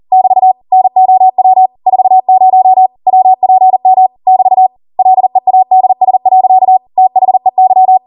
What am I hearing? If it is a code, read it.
Morse code, heard as a keyed tone, about 40 wpm, 750 Hz. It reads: <BT> NOW 40 WPM <BT> LEADS. THEY
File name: Morse / 40WPM